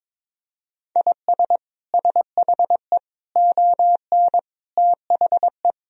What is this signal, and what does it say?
Morse code, heard as a keyed tone, about 22 wpm, 700 Hz. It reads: IS SHE ON THE
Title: Sentences from Top 1000 Words - Review Part 2 - 22wpm